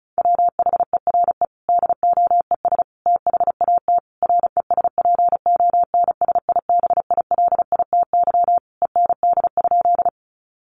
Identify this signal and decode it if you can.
Morse code, heard as a keyed tone, about 34 wpm, 700 Hz. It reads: WHERE DOES THAT RESPONSIBILITY END?
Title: Sentences from Top 1200 Words 35wpm